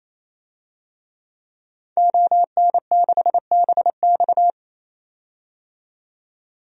Morse code, heard as a keyed tone, about 28 wpm, 700 Hz. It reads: ON6BX
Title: DXCC-100 Call Signs - 5 Characters - 28wpm